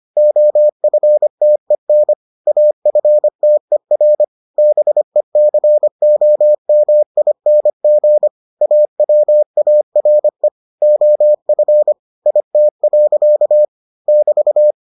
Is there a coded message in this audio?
Yes: Morse code from 0.2 to 14.7 s, heard as a keyed tone, about 25 wpm, 600 Hz. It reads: OFTEN AFTER BECOMING AWARE OF IT. <BT>